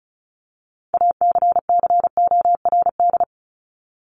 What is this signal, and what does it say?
Morse code, heard as a keyed tone, about 34 wpm, 700 Hz. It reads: ACCORD
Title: ING Suffix - Top 100 Words with Base Words 35wpm